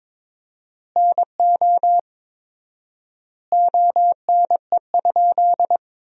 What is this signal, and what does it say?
Morse code, heard as a keyed tone, about 22 wpm, 700 Hz. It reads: NO ONE?
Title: 4x Word-Spacing - Sentences from Top 300 Words - 22wpm